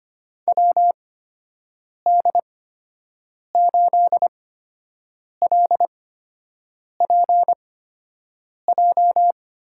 Morse code, heard as a keyed tone, about 25 wpm, spaced as for 7 wpm, 700 Hz. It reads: WD8LPJ
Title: USA Call Signs - 6 Characters - 8x Character Spacing - 25wpm